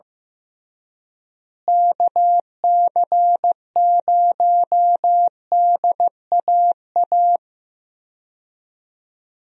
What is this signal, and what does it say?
Morse code, heard as a keyed tone, about 15 wpm, 700 Hz. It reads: KC0DAA
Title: USA Call Signs - 6 Characters - Repeated 3x - 15wpm